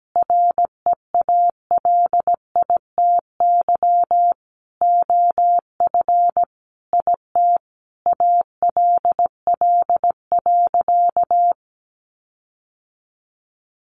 Morse code, heard as a keyed tone, about 17 wpm, 700 Hz. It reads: REALITY OF IT ALL.
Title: Sentences from Top 1500 Words 17wpm